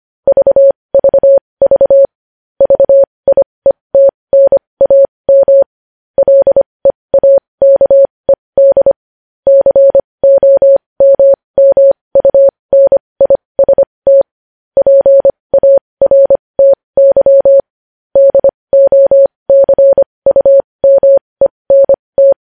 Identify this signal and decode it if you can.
Morse code, heard as a keyed tone, about 25 wpm, spaced as for 20 wpm, 550 Hz. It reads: VVV VIETNAM LEAKED COMMUNIST PARTY DOCUMENT
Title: News Headlines at 20